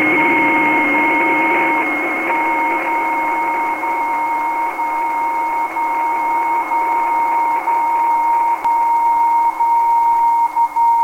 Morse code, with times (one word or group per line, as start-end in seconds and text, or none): none